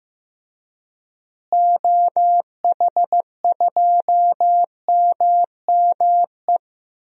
Morse code, heard as a keyed tone, about 15 wpm, 700 Hz. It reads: OH2MME
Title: DXCC-100 Call Signs - 6 Characters - Repeated 3x - 15wpm